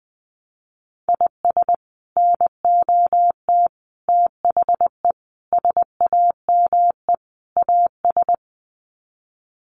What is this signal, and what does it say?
Morse code, heard as a keyed tone, about 20 wpm, 700 Hz. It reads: IS NOT THE SAME AS
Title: Sets of 5 Words Encore 20wpm